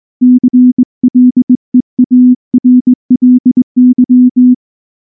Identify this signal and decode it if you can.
Morse code, heard as a keyed tone, about 19 wpm, 250 Hz. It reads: CLEARLY
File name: morse / morse-training-20WPM-8